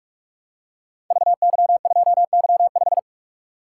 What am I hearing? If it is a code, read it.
Morse code, heard as a keyed tone, about 44 wpm, 700 Hz. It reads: VY2Y5
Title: Code Groups - 5 Letter:Numbers - 45wpm